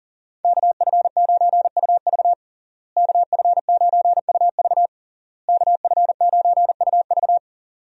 Morse code, heard as a keyed tone, about 40 wpm, 700 Hz. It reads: KF9UV KF9UV KF9UV
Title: USA Call Signs - 5 Characters - Repeated 3x - 40wpm